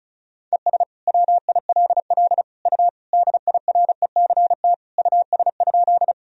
Morse code, heard as a keyed tone, about 36 wpm, 700 Hz. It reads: ES WILL U DIRECT US?
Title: Sentences from Top 400 Words and Abbreviations 35wpm